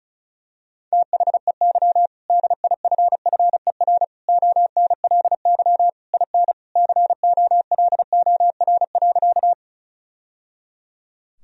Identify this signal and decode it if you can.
Morse code, heard as a keyed tone, about 34 wpm, 700 Hz. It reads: THEY DIFFER ONLY IN COLOR.
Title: Sentences from Top 1000 Words - Review Part 2 - 35wpm